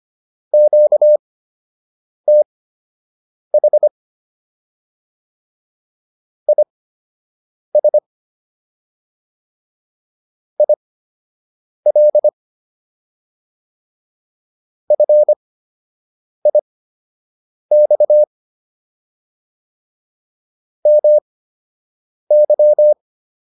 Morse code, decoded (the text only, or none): QTH IS IL FIX MY